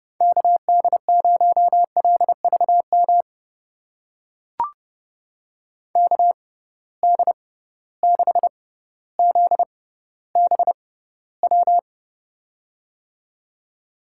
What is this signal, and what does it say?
Morse code, heard as a keyed tone, about 30 wpm, spaced as for 15 wpm, 700 Hz. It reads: KD0LVM KD6ZBW
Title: USA Call Signs - 6 Characters - 6x Character Spacing - 30wpm